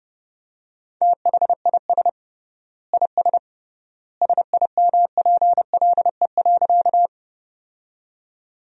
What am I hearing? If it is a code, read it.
Morse code, heard as a keyed tone, about 30 wpm, 700 Hz. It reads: THIS IS SIMPLE.
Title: 3x Word-Spacing - Sentences from Top 400 Words - 30wpm